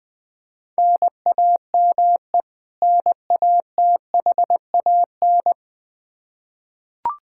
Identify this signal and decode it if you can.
Morse code, heard as a keyed tone, about 21 wpm, 700 Hz. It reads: NAME NATHAN
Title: Practice QSO - Exchange 1 - Beginning - 20wpm